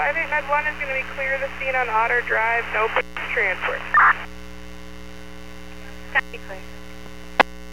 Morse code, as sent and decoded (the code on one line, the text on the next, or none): none